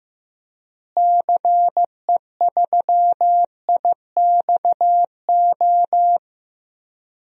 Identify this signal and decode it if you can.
Morse code, heard as a keyed tone, about 15 wpm, 700 Hz. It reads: CE3IXO
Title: DXCC-100 Call Signs - 6 Characters - Repeated 2x - 15wpm